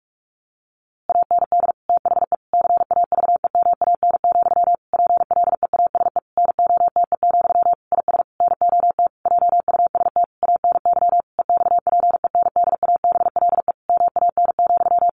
Morse code, heard as a keyed tone, about 44 wpm, 700 Hz. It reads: AND THE CAVEMAN, PLEASE NOTE, IS NOT JUST ANY EXPENDABLE MAN,
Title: A Sound Of Thunder 45wpm